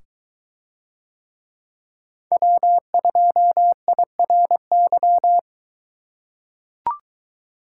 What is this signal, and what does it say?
Morse code, heard as a keyed tone, about 22 wpm, 700 Hz. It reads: W2IRY